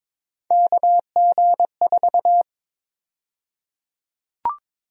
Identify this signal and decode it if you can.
Morse code, heard as a keyed tone, about 22 wpm, 700 Hz. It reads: KG4
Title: USA Call Sign Prefixes - Repeated 2x - 22wpm